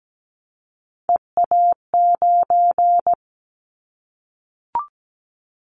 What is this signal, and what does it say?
Morse code, heard as a keyed tone, about 17 wpm, 700 Hz. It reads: EA9
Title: Callsign Prefixes for Top 100 DXCC - 17wpm